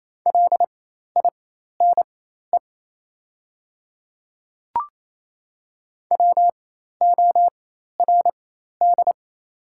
Morse code, heard as a keyed tone, about 28 wpm, spaced as for 13 wpm, 700 Hz. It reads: LINE WORD
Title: 4x Character Spacing - Top 300 Words - 28wpm